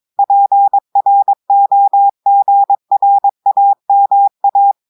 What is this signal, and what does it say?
Morse code, heard as a keyed tone, about 22 wpm, 800 Hz. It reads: PROGRAMA
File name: audios / portugais / portugais texte en clair 2 vitesse 22 mots minutes